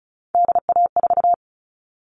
Morse code, heard as a keyed tone, about 34 wpm, 700 Hz. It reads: DA4
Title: All Cut Numbers - 3-Digit 35wpm